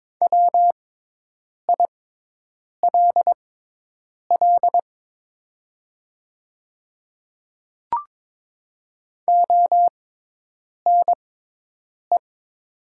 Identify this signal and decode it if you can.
Morse code, heard as a keyed tone, about 22 wpm, spaced as for 8 wpm, 700 Hz. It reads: WILL ONE
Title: 6x Character Spacing - Top 200 Words - 22wpm